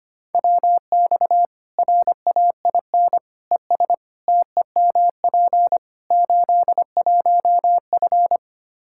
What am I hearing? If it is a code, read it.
Morse code, heard as a keyed tone, about 25 wpm, 700 Hz. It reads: WX RAIN ES TEMP 81F
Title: Practice QSO - Exchange 2 - Info - 25wpm